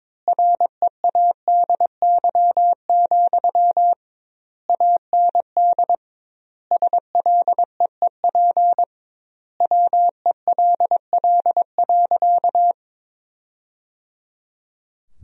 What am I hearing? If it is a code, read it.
Morse code, heard as a keyed tone, about 21 wpm, 700 Hz. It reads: READY, AND SLEEP WELL.
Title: 2x Word-Spacing - Sentences from Top 500 Words - 22wpm